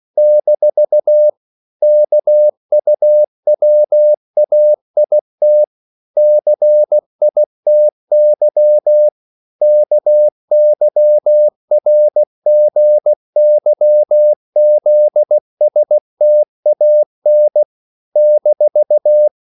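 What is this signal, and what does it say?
Morse code, heard as a keyed tone, about 16 wpm, 600 Hz. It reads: - KUWAIT CITY KYRGYZSTAN -
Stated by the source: CW practice of national capitals at 16 words per minute, from ON6ZQ.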